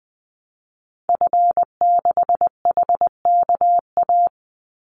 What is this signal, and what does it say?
Morse code, heard as a keyed tone, about 20 wpm, 700 Hz. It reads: F6HKA
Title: Call Signs Repeated 2x - 20wpm